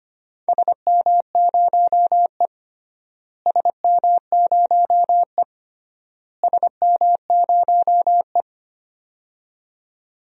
Morse code, heard as a keyed tone, about 25 wpm, 700 Hz. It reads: SM0E SM0E SM0E
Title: DXCC-100 Call Signs - 4 Characters - Repeated 3x - 25wpm